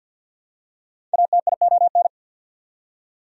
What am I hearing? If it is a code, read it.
Morse code, heard as a keyed tone, about 48 wpm, 690 Hz. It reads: ATION
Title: Prefixes and Suffixes 50wpm